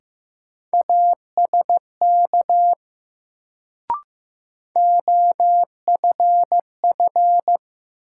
Morse code, heard as a keyed tone, about 15 wpm, 700 Hz. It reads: ASK OFF
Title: Sentences from Top 200 Words - New Word Drill - 15wpm